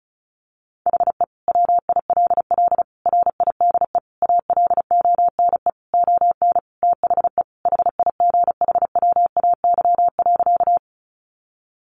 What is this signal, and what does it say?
Morse code, heard as a keyed tone, about 34 wpm, 700 Hz. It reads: HE WILL RIDE ALONE ON THE HIGHWAY.
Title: Sentences from Top 1400 Words 35wpm